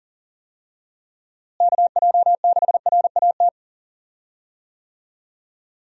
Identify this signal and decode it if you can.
Morse code, heard as a keyed tone, about 37 wpm, 700 Hz. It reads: KJ6RAT
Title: USA Call Signs - 6 Characters - Repeated 3x - 40wpm